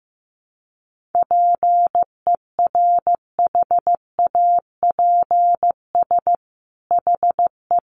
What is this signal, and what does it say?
Morse code, heard as a keyed tone, about 15 wpm, 700 Hz. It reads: PERHAPS HE
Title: Sentences from Top 500 Words 15wpm